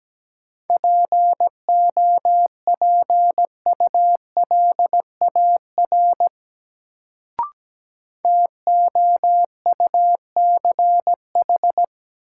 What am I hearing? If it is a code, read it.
Morse code, heard as a keyed tone, about 17 wpm, 700 Hz. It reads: POPULAR TOUCH
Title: Sentences from Top 1200 Words - New Word Drill Alt - 17wpm